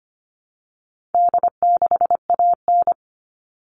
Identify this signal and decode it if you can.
Morse code, heard as a keyed tone, about 25 wpm, 700 Hz. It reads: D6AN
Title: All Cut Numbers - 4-Digit 25wpm